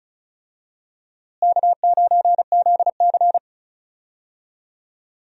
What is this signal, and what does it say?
Morse code, heard as a keyed tone, about 36 wpm, 700 Hz. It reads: K9ZC
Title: USA Call Signs - 4 Characters - Repeated 3x - 35wpm